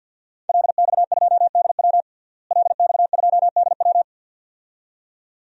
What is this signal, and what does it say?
Morse code, heard as a keyed tone, about 47 wpm, 700 Hz. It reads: LX2DW LX2DW
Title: DXCC-100 Call Signs - 5 Characters - Repeated 2x - 50wpm